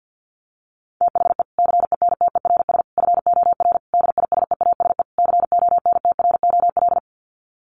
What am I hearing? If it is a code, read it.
Morse code, heard as a keyed tone, about 50 wpm, 700 Hz. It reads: THE CENTERS FOR DISEASE CONTROL
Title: Sets of 5 Words Encore2 50wpm